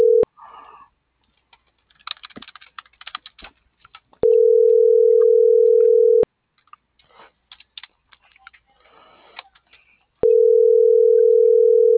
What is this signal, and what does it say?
Ringback tone (North America), 440 + 480 Hz.